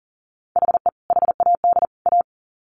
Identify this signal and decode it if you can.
Morse code, heard as a keyed tone, about 40 wpm, 700 Hz. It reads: HE HAD A
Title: Sets of 3 Words 40wpm